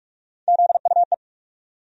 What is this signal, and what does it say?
Morse code, heard as a keyed tone, about 44 wpm, 700 Hz. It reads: ZUE
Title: USA Call Sign Postfixes - 3 Letters 45wpm